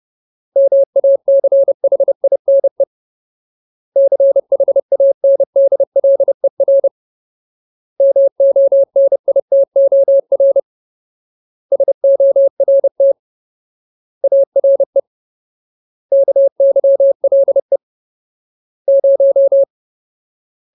Morse code, heard as a keyed tone, about 30 wpm, 550 Hz. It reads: MACHINE CHANDLER MONITOR SORT ARE KYLE 0
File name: Ultimate Word List / 30WPM Ultimate